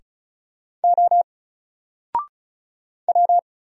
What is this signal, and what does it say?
Morse code, heard as a keyed tone, about 35 wpm, 700 Hz. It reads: O W
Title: Single Letter-Number 35wpm